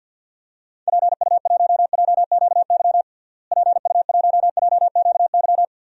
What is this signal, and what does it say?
Morse code, heard as a keyed tone, about 52 wpm, 700 Hz. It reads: PU1JQY PU1JQY
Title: DXCC-100 Call Signs - 6 Characters - Repeated 3x - 50wpm